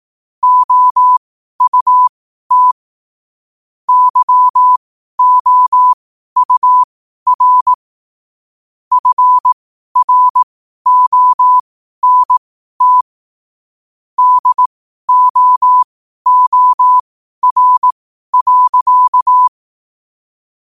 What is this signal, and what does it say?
Morse code, heard as a keyed tone, about 18 wpm, spaced as for 12 wpm, 1000 Hz. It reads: OUT YOUR FRONT DOOR.
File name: 12wpm